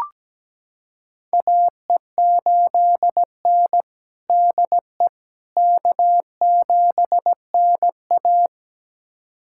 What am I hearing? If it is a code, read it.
Morse code, heard as a keyed tone, about 17 wpm, 700 Hz. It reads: AE8N DE K7NA
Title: USA Callsign DE Callsign - 4 Characters - 17wpm